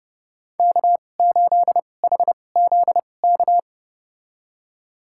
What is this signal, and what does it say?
Morse code, heard as a keyed tone, about 31 wpm, spaced as for 22 wpm, 700 Hz. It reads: K8HZK
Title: USA Call Signs - 5 Characters - 2x Character Spacing - 30wpm